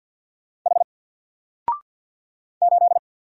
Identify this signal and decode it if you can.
Morse code, heard as a keyed tone, about 53 wpm, 700 Hz. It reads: H 8
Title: Single Letter-Number 50wpm